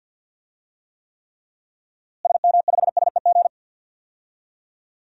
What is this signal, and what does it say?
Morse code, heard as a keyed tone, about 50 wpm, 700 Hz. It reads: SM5SEG
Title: DXCC-100 Call Signs - 6 Characters - Repeated 3x - 50wpm